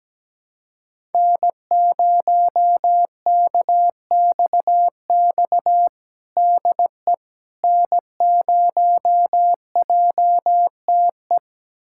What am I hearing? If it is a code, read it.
Morse code, heard as a keyed tone, about 17 wpm, 700 Hz. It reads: N0KXX DE N0JTE
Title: USA Callsign DE Callsign - 5 Characters - 17wpm